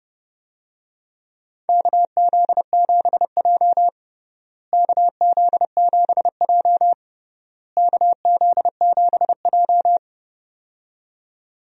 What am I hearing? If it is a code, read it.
Morse code, heard as a keyed tone, about 32 wpm, 700 Hz. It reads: KZ7J KZ7J KZ7J
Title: USA Call Signs - 4 Characters - Repeated 3x - 30wpm